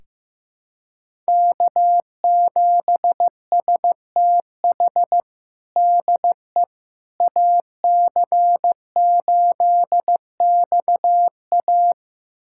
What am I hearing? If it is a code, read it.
Morse code, heard as a keyed tone, about 15 wpm, 700 Hz. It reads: K7STH DE AC8XA
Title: USA Callsign DE Callsign - 5 Characters - 15wpm